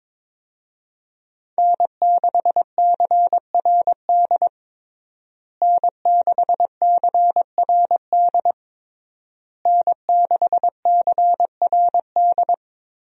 Morse code, heard as a keyed tone, about 22 wpm, 700 Hz. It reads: N6CRD N6CRD N6CRD
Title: USA Call Signs - 5 Characters - Repeated 3x - 22wpm